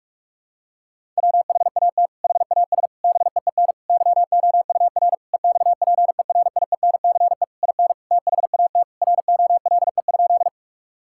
Morse code, heard as a keyed tone, about 47 wpm, 700 Hz. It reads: WHAT HAS BEEN YOUR EXPERIENCE IN THAT ROLE?
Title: Sentences from Top 1000 Words - Review Part 6 - 45wpm